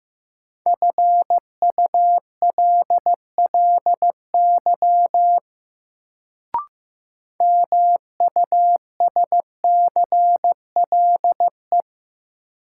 Morse code, heard as a keyed tone, about 15 wpm, 700 Hz. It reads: FULLY MUSCLE
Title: Sentences from Top 1300 Words - New Word Drill Alt - 15wpm